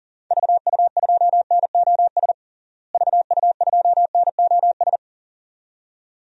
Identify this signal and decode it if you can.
Morse code, heard as a keyed tone, about 37 wpm, 700 Hz. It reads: VU2NOS VU2NOS
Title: DXCC-100 Call Signs - 6 Characters - Repeated 2x - 40wpm